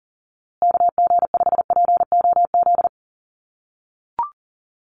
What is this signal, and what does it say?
Morse code, heard as a keyed tone, about 40 wpm, 700 Hz. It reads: KG5POZ
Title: USA Call Signs Repeated 2x 40wpm